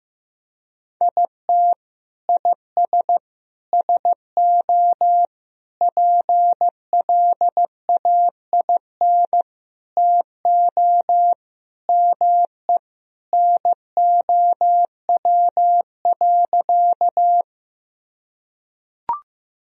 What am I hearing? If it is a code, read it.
Morse code, heard as a keyed tone, about 15 wpm, 700 Hz. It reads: IT IS SO PLAIN TO ME NOW.
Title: Sentences from Top 1000 Words - Review Part 2 - 15wpm